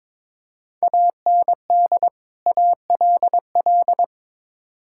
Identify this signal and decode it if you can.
Morse code, heard as a keyed tone, about 22 wpm, 700 Hz. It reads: AND ALL
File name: practice / 22wpm / Build A Phrase - 5-Word Phrases - 22wpm